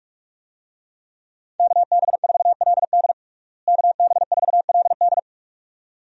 Morse code, heard as a keyed tone, about 46 wpm, 700 Hz. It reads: KB4LD KB4LD
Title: USA Call Signs - 5 Characters - Repeated 2x - 45wpm